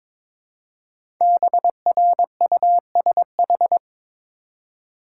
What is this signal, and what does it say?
Morse code, heard as a keyed tone, about 22 wpm, 700 Hz. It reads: BRUSH
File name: 5 Letter Words - First 3 Letters Given - 22wpm